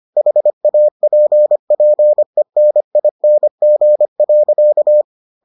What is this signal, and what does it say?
Morse code, heard as a keyed tone, about 25 wpm, 600 Hz. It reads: HAPPENING.